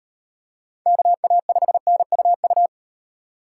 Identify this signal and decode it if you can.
Morse code, heard as a keyed tone, about 38 wpm, 700 Hz. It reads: KA5NUU